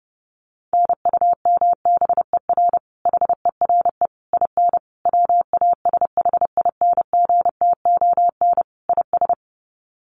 Morse code, heard as a keyed tone, about 30 wpm, 700 Hz. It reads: NUMBER HERE IN WASHINGTON IS
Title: Sets of 5 Words Encore2 30wpm